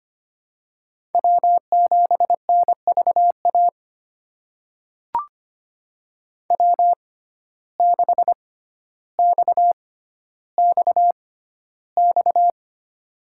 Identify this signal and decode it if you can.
Morse code, heard as a keyed tone, about 25 wpm, spaced as for 13 wpm, 700 Hz. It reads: W7NVA W6XXX
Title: USA Call Signs - 5 Characters - 6x Character Spacing - 25wpm